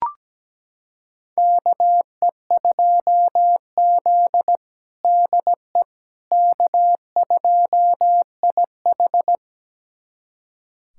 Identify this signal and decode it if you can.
Morse code, heard as a keyed tone, about 17 wpm, 700 Hz. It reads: KE2Z DE K2IH
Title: USA Callsign DE Callsign - 4 Characters - 17wpm